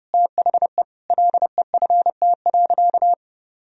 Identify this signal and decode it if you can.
Morse code, heard as a keyed tone, about 30 wpm, 700 Hz. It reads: THE LEFT.
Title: Sentences from Top 1000 Words - Review Part 1 - 30wpm